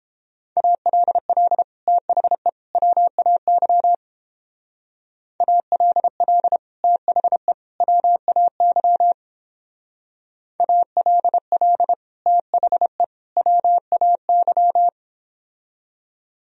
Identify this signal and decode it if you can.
Morse code, heard as a keyed tone, about 31 wpm, 700 Hz. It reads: ALL THE WAY ALL THE WAY ALL THE WAY